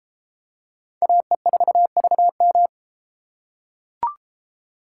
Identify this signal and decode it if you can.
Morse code, heard as a keyed tone, about 32 wpm, 700 Hz. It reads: AE4VM